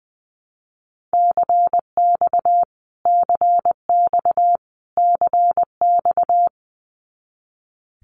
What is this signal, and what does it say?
Morse code, heard as a keyed tone, about 20 wpm, 700 Hz. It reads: CX CX CX
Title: Abbreviations Repeated 3x - 20wpm